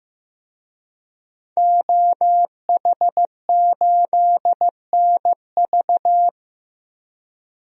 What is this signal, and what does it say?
Morse code, heard as a keyed tone, about 15 wpm, 700 Hz. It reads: OH8NV
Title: DXCC-100 Call Signs - 5 Characters - Repeated 3x - 15wpm